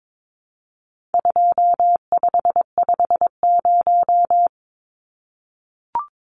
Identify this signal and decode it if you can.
Morse code, heard as a keyed tone, about 22 wpm, 700 Hz. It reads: 2550
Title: Numbers - 4-Digit 22wpm